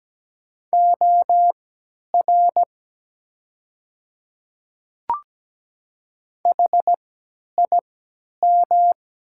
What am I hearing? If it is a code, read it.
Morse code, heard as a keyed tone, about 17 wpm, spaced as for 10 wpm, 700 Hz. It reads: OR HIM
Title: 3x Character Spacing - Top 200 Words - 17wpm